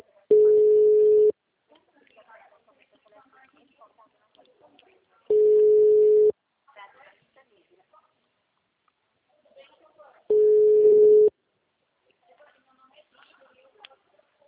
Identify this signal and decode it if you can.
Ringback tone (Europe, 425 Hz), with speech.
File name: out- -2021- - - .53323